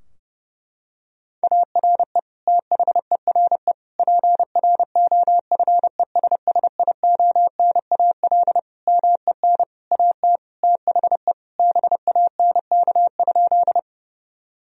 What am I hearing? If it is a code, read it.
Morse code, heard as a keyed tone, about 32 wpm, 700 Hz. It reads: ARE THERE PROFESSIONAL MEN AT THE BANK?
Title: Sentences from Top 1000 Words - Review Part 4 - 30wpm